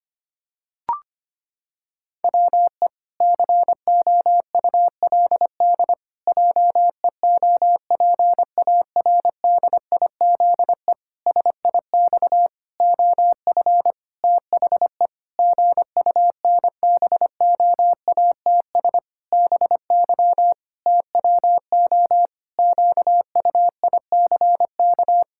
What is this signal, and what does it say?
Morse code, heard as a keyed tone, about 25 wpm, 700 Hz. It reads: WE COULD JEOPARDIZE SIX OF THE GUNBOATS BY TWO QUICK
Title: Pangram Challenge v2 - 25wpm